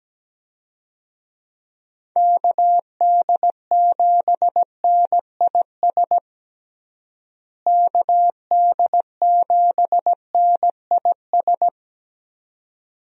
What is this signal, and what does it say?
Morse code, heard as a keyed tone, about 17 wpm, 700 Hz. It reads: KD7NIS KD7NIS
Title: USA Call Signs - 6 Characters - Repeated 2x - 17wpm